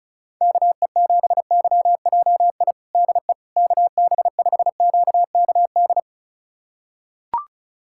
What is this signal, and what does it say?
Morse code, heard as a keyed tone, about 34 wpm, 700 Hz. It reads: KE7YJI DE KB5QKD
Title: USA Callsign DE Callsign - 6 Characters - 35wpm